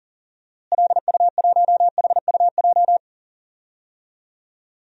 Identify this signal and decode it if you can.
Morse code, heard as a keyed tone, about 37 wpm, 700 Hz. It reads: LU1HUJ
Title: DXCC-100 Call Signs - 6 Characters - Repeated 3x - 40wpm